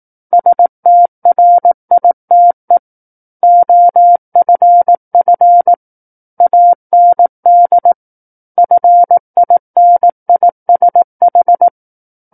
Morse code, heard as a keyed tone, about 18 wpm, 700 Hz. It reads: STRITE OFF AND FINISH
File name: 18 WPM